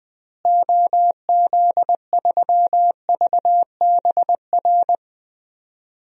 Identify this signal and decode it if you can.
Morse code, heard as a keyed tone, about 20 wpm, 700 Hz. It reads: OZ3VBR
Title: DXCC-100 Call Signs - 6 Characters - Repeated 2x - 20wpm